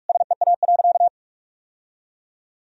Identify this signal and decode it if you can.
Morse code, heard as a keyed tone, about 41 wpm, 700 Hz. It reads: SEA.
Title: Sentences from Top 1000 Words - Review Part 1 - 45wpm